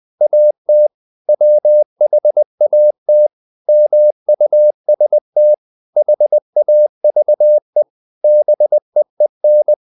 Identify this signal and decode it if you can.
Morse code, heard as a keyed tone, about 20 wpm, 600 Hz. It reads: AT WHAT MUST HAVE BEEN
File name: the old man and the sea part 2 20wpm